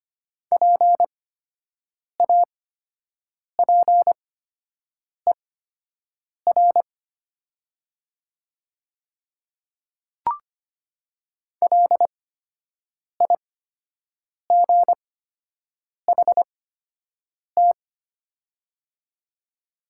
Morse code, heard as a keyed tone, about 25 wpm, spaced as for 7 wpm, 700 Hz. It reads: PAPER LIGHT
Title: 8x Character Spacing - Top 300 Words - 25wpm